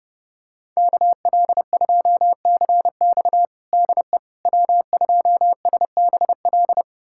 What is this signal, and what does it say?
Morse code, heard as a keyed tone, about 30 wpm, 700 Hz. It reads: KL2CX DE W2SBL
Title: USA Callsign DE Callsign - 5 Characters - 30wpm